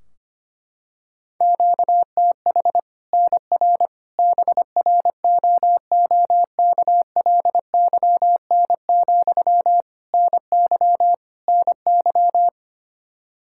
Morse code, heard as a keyed tone, about 25 wpm, 700 Hz. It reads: QTH NR BROOKLYN, NY NY
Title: Practice QSO - Exchange 1 - Beginning - 25wpm